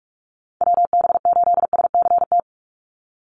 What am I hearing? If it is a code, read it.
Morse code, heard as a keyed tone, about 44 wpm, 700 Hz. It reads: WB8SCT
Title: Call Signs - 45wpm